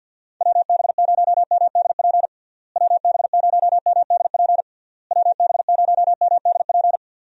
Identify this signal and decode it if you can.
Morse code, heard as a keyed tone, about 50 wpm, 700 Hz. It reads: WB0MDP WB0MDP WB0MDP
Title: USA Call Signs - 6 Characters - Repeated 3x - 50wpm